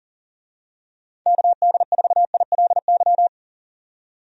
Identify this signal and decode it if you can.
Morse code, heard as a keyed tone, about 37 wpm, 700 Hz. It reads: KD4ILY
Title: USA Call Signs - 6 Characters - 40wpm